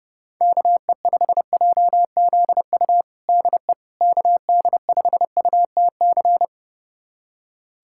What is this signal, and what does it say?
Morse code, heard as a keyed tone, about 32 wpm, 700 Hz. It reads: KE5JZU DE KD5UTC
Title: USA Callsign DE Callsign 30wpm